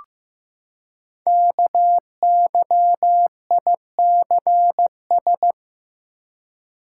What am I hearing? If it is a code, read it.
Morse code, heard as a keyed tone, about 15 wpm, 700 Hz. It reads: KYICS
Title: Code Groups - 5 Letters - 15wpm